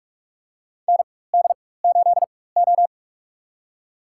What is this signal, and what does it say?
Morse code, heard as a keyed tone, about 43 wpm, spaced as for 21 wpm, 700 Hz. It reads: ND8O
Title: USA Call Signs - 4 Characters - 4x Character Spacing - 45wpm